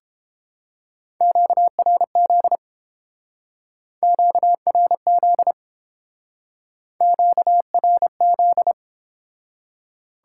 Morse code, heard as a keyed tone, about 29 wpm, 700 Hz. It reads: QRZ QRZ QRZ